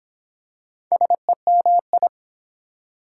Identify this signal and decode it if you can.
Morse code, heard as a keyed tone, about 23 wpm, 700 Hz. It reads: SEMI